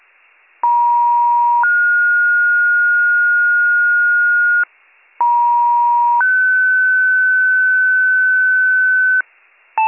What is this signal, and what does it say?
Two-tone page, heard as a one-second tone, then a three-second tone, about 950 then 1430 Hz, about 950 then 1500 Hz, with noise. It is heard from 0.6 to 9.2 s.